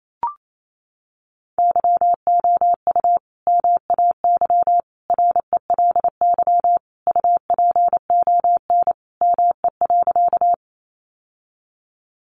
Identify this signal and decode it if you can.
Morse code, heard as a keyed tone, about 28 wpm, 700 Hz. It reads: YOU MAY RELY UPON ME.
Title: Sentences from Top 1200 Words 28wpm